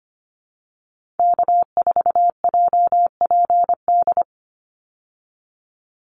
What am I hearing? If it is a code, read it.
Morse code, heard as a keyed tone, about 25 wpm, 700 Hz. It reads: K4JPD
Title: Call Signs Repeated 2x - 25wpm